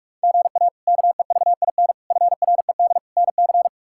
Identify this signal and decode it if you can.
Morse code, heard as a keyed tone, about 46 wpm, 700 Hz. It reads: GA KEVIN FRED NC
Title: K1USN SST QSO Practice - 45wpm